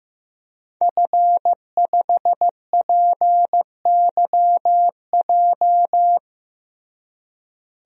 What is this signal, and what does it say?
Morse code, heard as a keyed tone, about 15 wpm, 700 Hz. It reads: F5PYJ
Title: DXCC-100 Call Signs - 5 Characters - Repeated 2x - 15wpm